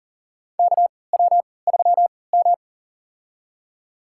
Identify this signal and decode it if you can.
Morse code, heard as a keyed tone, about 40 wpm, spaced as for 23 wpm, 700 Hz. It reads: KW3M
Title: USA Call Signs - 3x Character Spacing - 40wpm